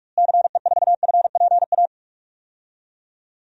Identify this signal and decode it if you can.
Morse code, heard as a keyed tone, about 46 wpm, 700 Hz. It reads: CE4FPA
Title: DXCC-100 Call Signs - 6 Characters - Repeated 2x - 45wpm